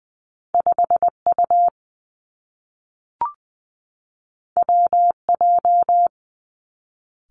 Morse code, heard as a keyed tone, about 20 wpm, 700 Hz. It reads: 5 U W J
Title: Two Challenging Letter-Number Combinations 20wpm